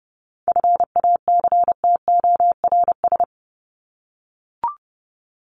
Morse code, heard as a keyed tone, about 30 wpm, 700 Hz. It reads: FACTORS
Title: ES:S Suffix - Top 100 Words with Base Words 30wpm